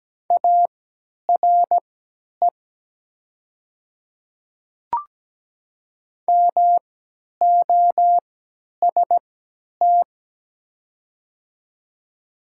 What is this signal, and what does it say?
Morse code, heard as a keyed tone, about 17 wpm, spaced as for 10 wpm, 700 Hz. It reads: ARE MOST
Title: 3x Character Spacing - Top 100 Words - 17wpm